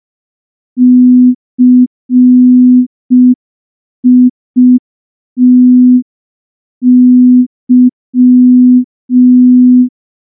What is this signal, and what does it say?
Morse code, heard as a keyed tone, about 5 wpm, 250 Hz. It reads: CITY
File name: morse / morse-training-05WPM